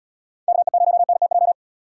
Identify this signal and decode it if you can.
Morse code, heard as a keyed tone, about 76 wpm, 700 Hz. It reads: KE0NTJ